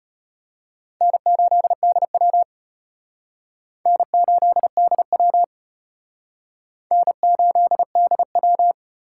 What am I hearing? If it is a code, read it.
Morse code, heard as a keyed tone, about 34 wpm, 700 Hz. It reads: N8DW N8DW N8DW